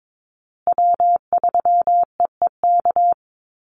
Morse code, heard as a keyed tone, about 22 wpm, 700 Hz. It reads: W3EEK
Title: Call Signs Repeated 3x - 22wpm